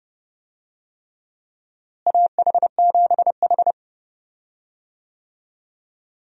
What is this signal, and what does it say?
Morse code, heard as a keyed tone, about 32 wpm, 700 Hz. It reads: AH7H
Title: USA Call Signs - 4 Characters - Repeated 3x - 30wpm